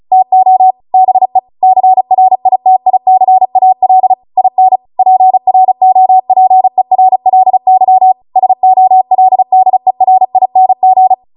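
Morse code, heard as a keyed tone, about 36 wpm, 750 Hz. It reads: TO BE CRITICAL IN PROPERLY SOLDERING
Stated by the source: Amateur radio Morse code practice archive transmission at 35 wpm.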